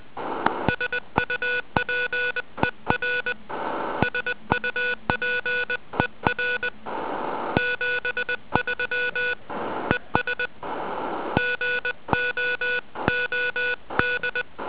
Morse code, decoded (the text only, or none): SUPER SUPER 73 ES GOOD